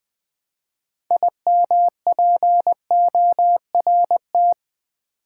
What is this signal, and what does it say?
Morse code, heard as a keyed tone, about 20 wpm, 700 Hz. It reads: IMPORT
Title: Build A Word - 2 Letters At A Time - 6-Letter Words - 20wpm